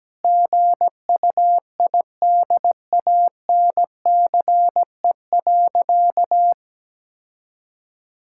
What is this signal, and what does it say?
Morse code, heard as a keyed tone, about 17 wpm, 700 Hz. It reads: GUIDANCE.
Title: Sentences from Top 1600 Words 17wpm